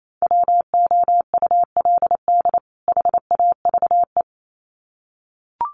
Morse code, heard as a keyed tone, about 28 wpm, 700 Hz. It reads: WOULD HAVE
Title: Sets of 2 Words 28wpm